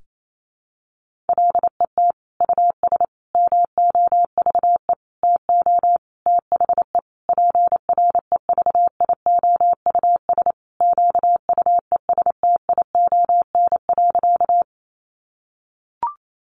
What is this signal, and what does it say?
Morse code, heard as a keyed tone, about 28 wpm, 700 Hz. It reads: LET US MOVE TO THE PREVIOUS QUESTION.
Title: Sentences from Top 1500 Words 28wpm